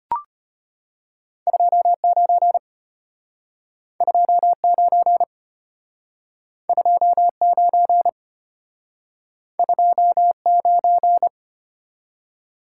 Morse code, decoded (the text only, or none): 29 29 29 29